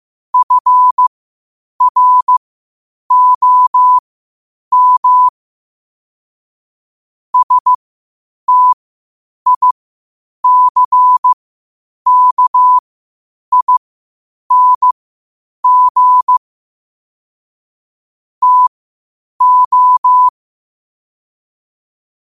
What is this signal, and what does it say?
Morse code, heard as a keyed tone, about 15 wpm, spaced as for 8 wpm, 1000 Hz. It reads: FROM STICKING TO
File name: qotdfiles / 8wpm